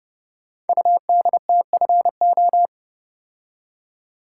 Morse code, heard as a keyed tone, about 28 wpm, 700 Hz. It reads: UDTFO
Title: Code Groups - 5 Letters - 30wpm